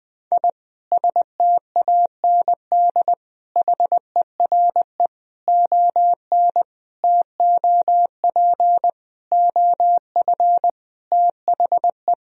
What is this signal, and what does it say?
Morse code, heard as a keyed tone, about 20 wpm, 700 Hz. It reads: I STAND HERE ON TOP OF THE
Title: Sentences from Top 1000 Words - Review Part 1 - 20wpm